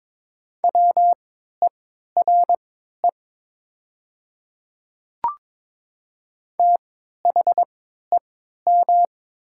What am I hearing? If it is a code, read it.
Morse code, heard as a keyed tone, about 22 wpm, spaced as for 12 wpm, 700 Hz. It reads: WERE THEM
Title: 3x Character Spacing - Top 100 Words - 22wpm